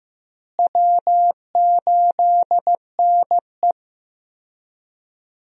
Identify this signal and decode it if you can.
Morse code, heard as a keyed tone, about 16 wpm, 700 Hz. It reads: W8NE
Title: USA Call Signs - 4 Characters - Repeated 3x - 15wpm